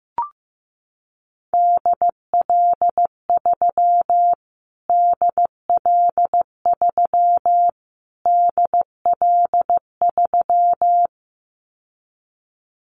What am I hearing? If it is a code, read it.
Morse code, heard as a keyed tone, about 15 wpm, 700 Hz. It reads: DL3 DL3 DL3
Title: Callsign Prefixes for Top 25 DXCC - Repeated 3x - 15wpm